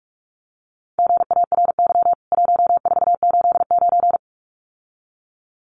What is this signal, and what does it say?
Morse code, heard as a keyed tone, about 45 wpm, 700 Hz. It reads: GARY 1489
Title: CWT Exchange 45wpm